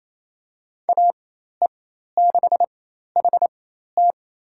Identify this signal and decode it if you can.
Morse code, heard as a keyed tone, about 28 wpm, spaced as for 13 wpm, 700 Hz. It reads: AE6HT
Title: USA Call Signs - 5 Characters - 4x Character Spacing - 28wpm